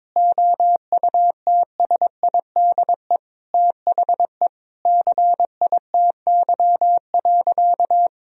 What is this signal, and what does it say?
Morse code, heard as a keyed tone, about 21 wpm, 700 Hz. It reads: OUTSIDE THE CITY.
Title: Sentences from Top 1000 Words - Review Part 4 - 22wpm